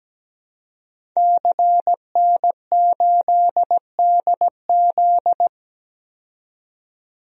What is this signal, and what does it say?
Morse code, heard as a keyed tone, about 17 wpm, 700 Hz. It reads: CN8DZ
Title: DXCC-100 Call Signs - 5 Characters - 17wpm